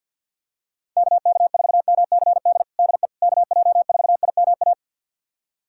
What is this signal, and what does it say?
Morse code, heard as a keyed tone, about 52 wpm, 700 Hz. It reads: KK4MCD DE KJ4IMA
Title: USA Callsign DE Callsign - 6 Characters - 50wpm